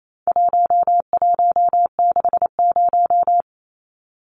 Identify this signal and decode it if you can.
Morse code, heard as a keyed tone, about 28 wpm, 700 Hz. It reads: 1160
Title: Numbers - 4-Digit 28wpm